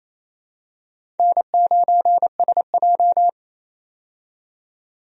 Morse code, heard as a keyed tone, about 28 wpm, 700 Hz. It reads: N9SJ
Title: USA Call Signs - 4 Characters - 28wpm